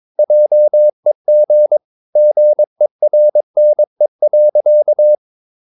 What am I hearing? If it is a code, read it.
Morse code, heard as a keyed tone, about 22 wpm, 600 Hz. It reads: JEG GERNE.
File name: Ferie QRS 22wpm